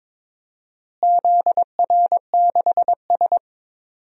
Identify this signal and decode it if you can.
Morse code, heard as a keyed tone, about 22 wpm, 700 Hz. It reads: ZR6S
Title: DXCC-100 Call Signs - 4 Characters - Repeated 3x - 22wpm